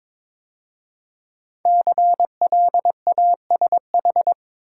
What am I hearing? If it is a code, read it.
Morse code, heard as a keyed tone, about 22 wpm, 700 Hz. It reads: CLASH
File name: 5 Letter Words - First 3 Letters Given - 22wpm